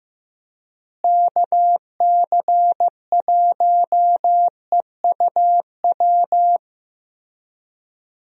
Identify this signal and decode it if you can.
Morse code, heard as a keyed tone, about 16 wpm, 700 Hz. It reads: KC1EUW
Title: USA Call Signs - 6 Characters - Repeated 3x - 15wpm